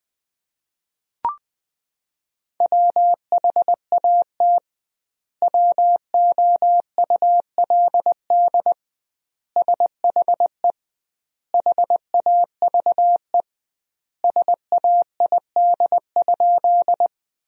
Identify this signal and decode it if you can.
Morse code, heard as a keyed tone, about 21 wpm, 700 Hz. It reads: WHAT WOULD SHE HAVE SAID?
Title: 2x Word-Spacing - Sentences from Top 100 Words - 20wpm